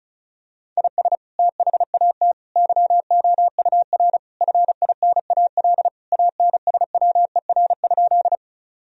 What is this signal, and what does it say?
Morse code, heard as a keyed tone, about 34 wpm, 700 Hz. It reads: IS THAT YOUR FINAL ANSWER?
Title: Sentences from Top 1000 Words - Review Part 2 - 35wpm